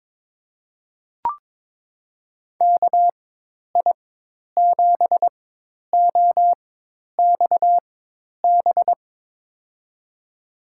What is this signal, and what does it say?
Morse code, heard as a keyed tone, about 22 wpm, spaced as for 10 wpm, 700 Hz. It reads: KI7OXB
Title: USA Call Signs - 4x Character Spacing - 22wpm